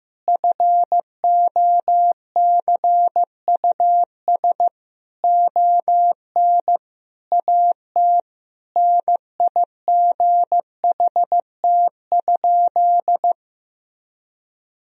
Morse code, heard as a keyed tone, about 15 wpm, 700 Hz. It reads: FOCUS ON AT NIGHT?
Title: Sentences from Top 1000 Words - Review Part 4 - 15wpm